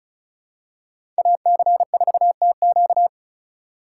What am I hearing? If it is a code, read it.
Morse code, heard as a keyed tone, about 34 wpm, 700 Hz. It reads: AC4TQ
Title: USA Call Signs - 4x Character Spacing - 35wpm